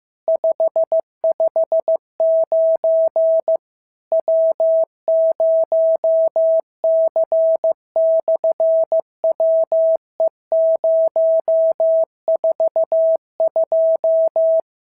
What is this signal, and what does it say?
Morse code, heard as a keyed tone, about 15 wpm, 650 Hz. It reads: 559 W0C/WE042
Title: SOTA QSO Practice 15wpm